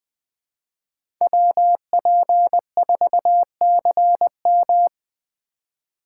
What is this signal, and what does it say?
Morse code, heard as a keyed tone, about 21 wpm, 700 Hz. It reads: WP4CM
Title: USA Call Signs - 5 Characters - 3x Character Spacing - 20wpm